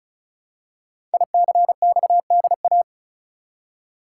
Morse code, heard as a keyed tone, about 36 wpm, 700 Hz. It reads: ICXDA
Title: Code Groups - 5 Letter:Numbers - 35wpm